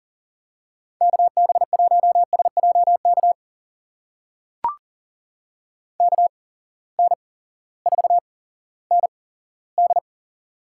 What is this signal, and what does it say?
Morse code, heard as a keyed tone, about 40 wpm, spaced as for 18 wpm, 700 Hz. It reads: KB1SJK KN4ND
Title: USA Call Signs - 6 Characters - 8x Character Spacing - 40wpm